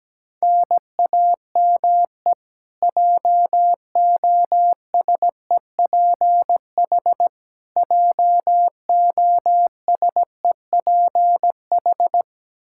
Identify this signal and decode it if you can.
Morse code, heard as a keyed tone, about 17 wpm, 700 Hz. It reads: NAME JOSEPH JOSEPH
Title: Practice QSO - Exchange 1 - Beginning - 17wpm